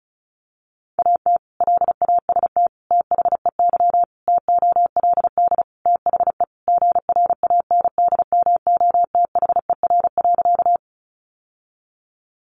Morse code, heard as a keyed tone, about 34 wpm, 700 Hz. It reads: AT LAST THEY TOLD THE GRANDMOTHER.
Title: Sentences from Top 1200 Words 35wpm